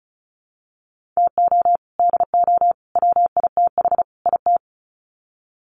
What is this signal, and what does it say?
Morse code, heard as a keyed tone, about 34 wpm, 700 Hz. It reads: TO DO WITH IT
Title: Sets of 4 Words 35wpm